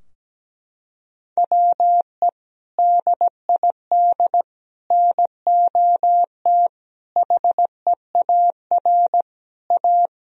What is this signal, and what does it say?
Morse code, heard as a keyed tone, about 17 wpm, 700 Hz. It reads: WE DID NOT HEAR A
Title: Sentences from Top 1000 Words - Review Part 4 - 17wpm